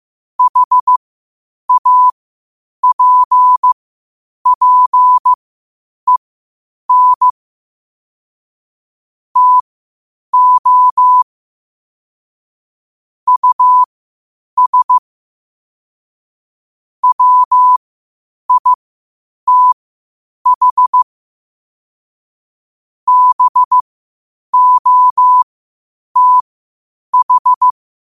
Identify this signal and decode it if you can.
Morse code, heard as a keyed tone, about 15 wpm, spaced as for 8 wpm, 1000 Hz. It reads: HAPPEN TO US WITH BOTH
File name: qotdfiles / 8wpm